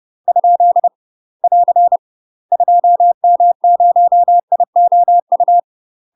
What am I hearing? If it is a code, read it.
Morse code, heard as a keyed tone, about 30 wpm, 700 Hz. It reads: ? <AR> 2M0IOU
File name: UK MASTER QSOs TRAINING / 30WPM / UK MASTER QSOs 0038